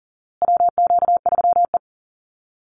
Morse code, heard as a keyed tone, about 40 wpm, 700 Hz. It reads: WQ3E
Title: Call Signs Repeated 2x - 40wpm